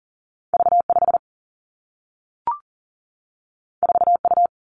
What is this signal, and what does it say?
Morse code, heard as a keyed tone, about 42 wpm, 700 Hz. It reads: V 5 4 U
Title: Two Challenging Letter-Number Combinations 40wpm